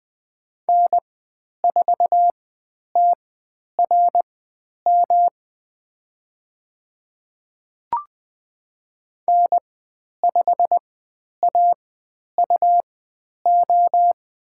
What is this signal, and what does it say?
Morse code, heard as a keyed tone, about 20 wpm, spaced as for 10 wpm, 700 Hz. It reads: N4TRM N5AUO
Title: 109 Lesson 18g - Letter m - Call Signs 20wpm